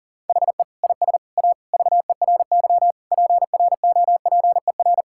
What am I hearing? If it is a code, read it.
Morse code, heard as a keyed tone, about 37 wpm, 700 Hz. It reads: HE IS A VERY PROPER
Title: Sentences from Top 1000 Words - Review Part 5 - 40wpm